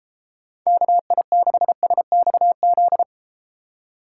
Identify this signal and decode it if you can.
Morse code, heard as a keyed tone, about 31 wpm, 700 Hz. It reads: KI6SXZ